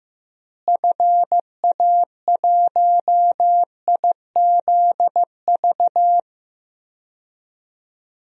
Morse code, heard as a keyed tone, about 15 wpm, 700 Hz. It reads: FA1IZV
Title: DXCC-100 Call Signs - 6 Characters - Repeated 3x - 15wpm